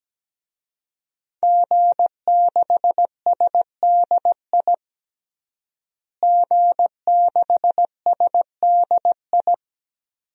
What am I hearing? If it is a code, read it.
Morse code, heard as a keyed tone, about 17 wpm, 700 Hz. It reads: G6SDI G6SDI
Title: DXCC-100 Call Signs - 5 Characters - Repeated 2x - 17wpm